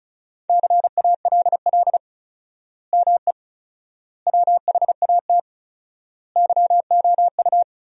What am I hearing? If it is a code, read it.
Morse code, heard as a keyed tone, about 36 wpm, 700 Hz. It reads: CALL ME WHAT YOU
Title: 4x Word-Spacing - Sentences from Top 200 Words - 35wpm